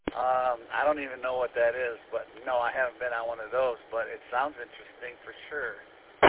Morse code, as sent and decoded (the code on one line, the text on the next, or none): none